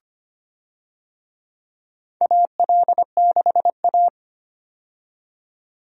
Morse code, heard as a keyed tone, about 25 wpm, 700 Hz. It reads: AL6A
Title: USA Call Signs - 4 Characters - 25wpm